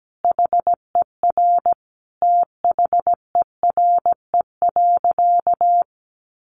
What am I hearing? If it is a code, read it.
Morse code, heard as a keyed tone, about 17 wpm, 700 Hz. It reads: HER THERE.
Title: Sentences from Top 1500 Words 17wpm